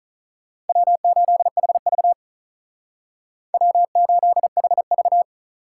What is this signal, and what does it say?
Morse code, heard as a keyed tone, about 45 wpm, 700 Hz. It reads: W8HV W8HV